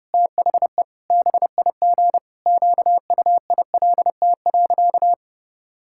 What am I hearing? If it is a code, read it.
Morse code, heard as a keyed tone, about 30 wpm, 700 Hz. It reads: THE BIG QUILT.
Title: Pangram Challenge v2 - 30wpm